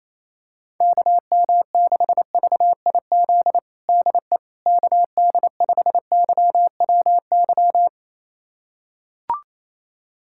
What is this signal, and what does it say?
Morse code, heard as a keyed tone, about 28 wpm, 700 Hz. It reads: KM6VIZ DE KD5YWY
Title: USA Callsign DE Callsign - 6 Characters - 28wpm